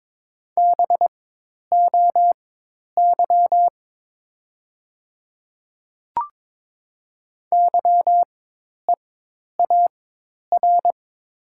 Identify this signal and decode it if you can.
Morse code, heard as a keyed tone, about 22 wpm, spaced as for 10 wpm, 700 Hz. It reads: BOY YEAR
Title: 4x Character Spacing - Top 200 Words - 22wpm